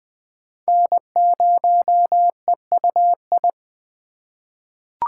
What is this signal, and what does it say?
Morse code, heard as a keyed tone, about 20 wpm, 700 Hz. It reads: N0EUI
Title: USA Call Signs - 5 Characters - 4x Character Spacing - 20wpm